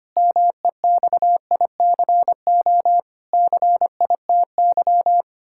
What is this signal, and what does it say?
Morse code, heard as a keyed tone, about 25 wpm, 700 Hz. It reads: MEXICO CITY
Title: Pangram Challenge v2 - 25wpm